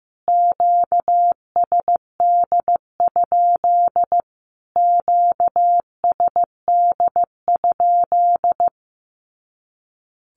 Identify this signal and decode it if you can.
Morse code, heard as a keyed tone, about 15 wpm, 700 Hz. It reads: QSD? QSD?
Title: Q-Codes Repeated 3x 15wpm